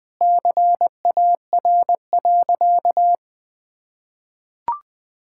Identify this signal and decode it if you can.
Morse code, heard as a keyed tone, about 20 wpm, 700 Hz. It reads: CAR.
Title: Sentences with 4-Word Phrases - 20wpm